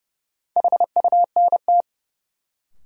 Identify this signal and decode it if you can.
Morse code, heard as a keyed tone, about 30 wpm, 700 Hz. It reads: HUNT
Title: Sentences from Top 1600 Words - New Word Drill - 30wpm